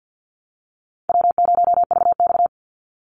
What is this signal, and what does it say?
Morse code, heard as a keyed tone, about 50 wpm, 700 Hz. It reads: W0VX
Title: Easy Call Signs Repeated 3x - 50wpm